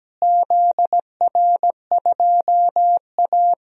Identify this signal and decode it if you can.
Morse code, heard as a keyed tone, about 17 wpm, 700 Hz. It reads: ZR2A
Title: DXCC-100 Call Signs - 4 Characters - Repeated 2x - 17wpm